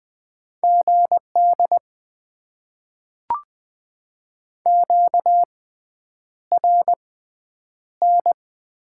Morse code, heard as a keyed tone, about 20 wpm, spaced as for 9 wpm, 700 Hz. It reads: GD QRN
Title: Top 90 QSO Elements - 6x Character Spacing - 20wpm